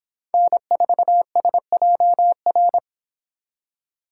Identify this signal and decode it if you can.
Morse code, heard as a keyed tone, about 26 wpm, 700 Hz. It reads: N4SJR